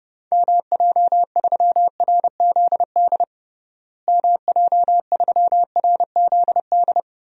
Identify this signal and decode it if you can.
Morse code, heard as a keyed tone, about 30 wpm, 700 Hz. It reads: MJ3RZD MJ3RZD
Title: DXCC-100 Call Signs - 6 Characters - Repeated 3x - 30wpm